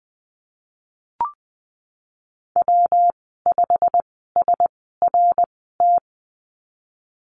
Morse code, heard as a keyed tone, about 20 wpm, spaced as for 14 wpm, 700 Hz. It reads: W5SRT
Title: 140 Lesson 23g - Letter f - Call Signs 20wpm